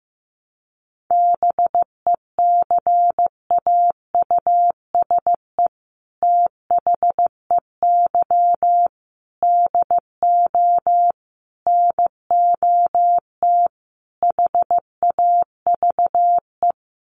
Morse code, heard as a keyed tone, about 15 wpm, 700 Hz. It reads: BECAUSE THEY DO NOT HAVE
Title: Sets of 5 Words Encore2 15wpm